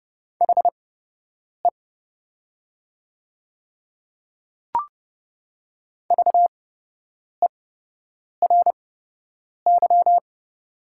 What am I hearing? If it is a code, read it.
Morse code, heard as a keyed tone, about 30 wpm, spaced as for 8 wpm, 700 Hz. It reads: HE VERY
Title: 8x Character Spacing - Top 500 Words - 30wpm